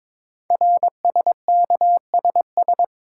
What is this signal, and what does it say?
Morse code, heard as a keyed tone, about 22 wpm, 700 Hz. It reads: RSKSS
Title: Code Groups - 5 Letters - 22wpm